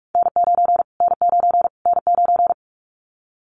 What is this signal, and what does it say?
Morse code, heard as a keyed tone, about 44 wpm, 700 Hz. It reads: N9 N9 N9
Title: Callsign Prefixes for Top 25 DXCC - Repeated 3x - 45wpm